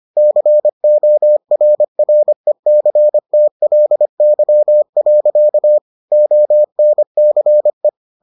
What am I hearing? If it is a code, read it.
Morse code, heard as a keyed tone, about 25 wpm, 600 Hz. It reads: CORRECTLY. ONCE